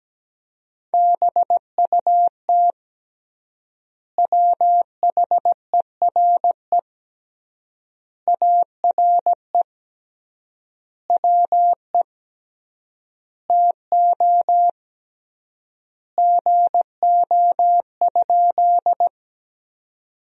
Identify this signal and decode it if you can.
Morse code, heard as a keyed tone, about 17 wpm, 700 Hz. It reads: BUT WHERE ARE WE TO GO?
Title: 3x Word-Spacing - Sentences from Top 200 Words - 17wpm